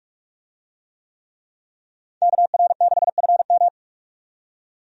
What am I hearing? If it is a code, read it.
Morse code, heard as a keyed tone, about 42 wpm, 700 Hz. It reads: KR6FM
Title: USA Call Signs - 5 Characters - 45wpm